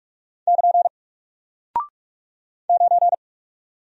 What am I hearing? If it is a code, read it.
Morse code, heard as a keyed tone, about 44 wpm, 700 Hz. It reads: <KN> 9
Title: Single Character (Letter:Number:Punctuation:Prosign) Alt 45wpm